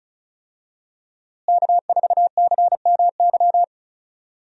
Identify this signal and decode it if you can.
Morse code, heard as a keyed tone, about 36 wpm, 700 Hz. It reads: K4CMY
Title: USA Call Signs - 5 Characters - Repeated 2x - 35wpm